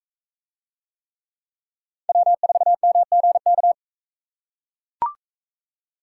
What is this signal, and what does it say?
Morse code, heard as a keyed tone, about 38 wpm, 700 Hz. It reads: W4MGK